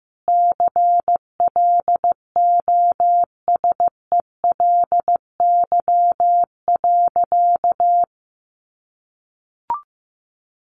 Morse code, heard as a keyed tone, about 15 wpm, 700 Hz. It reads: CLOSELY.
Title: Sentences from Top 1400 Words 15wpm